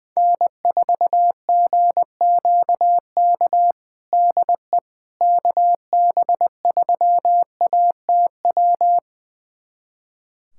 Morse code, heard as a keyed tone, about 20 wpm, 700 Hz. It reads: N4GQK DE KB3ATW
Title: USA Callsign DE Callsign 20wpm